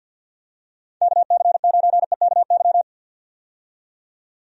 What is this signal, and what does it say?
Morse code, heard as a keyed tone, about 49 wpm, 700 Hz. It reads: KC9EKY
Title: USA Call Signs - 6 Characters - 50wpm